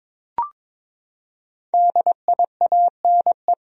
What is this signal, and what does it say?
Morse code, heard as a keyed tone, about 22 wpm, 700 Hz. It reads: DIANE
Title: USA Common Names - Max 5 Letters - 22wpm